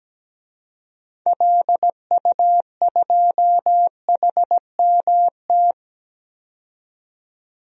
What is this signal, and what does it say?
Morse code, heard as a keyed tone, about 17 wpm, 700 Hz. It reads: LU2HMT
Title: DXCC-100 Call Signs - 6 Characters - 17wpm